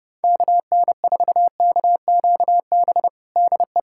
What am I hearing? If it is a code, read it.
Morse code, heard as a keyed tone, about 32 wpm, 700 Hz. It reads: KN4KQB DE
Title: USA Callsign DE Callsign 30wpm